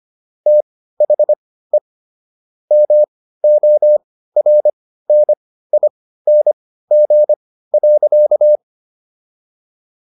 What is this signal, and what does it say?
Morse code, heard as a keyed tone, about 25 wpm, spaced as for 15 wpm, 600 Hz. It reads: THE MORNING.